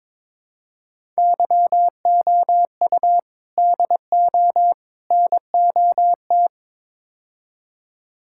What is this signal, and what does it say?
Morse code, heard as a keyed tone, about 22 wpm, 700 Hz. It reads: YOU DO NOT
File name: practice / 22wpm / Build A Phrase - 5-Word Phrases - 22wpm